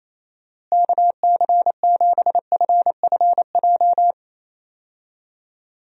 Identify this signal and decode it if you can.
Morse code, heard as a keyed tone, about 27 wpm, 700 Hz. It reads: KC7FFJ
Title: USA Call Signs - 6 Characters - Repeated 2x - 28wpm